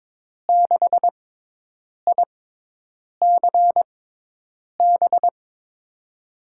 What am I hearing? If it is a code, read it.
Morse code, heard as a keyed tone, about 22 wpm, spaced as for 8 wpm, 700 Hz. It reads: 6ICB
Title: USA Call Signs - 6x Character Spacing - 22wpm